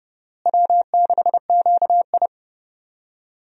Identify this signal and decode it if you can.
Morse code, heard as a keyed tone, about 30 wpm, 700 Hz. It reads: W6QI
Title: USA Call Signs - 4 Characters - Repeated 3x - 30wpm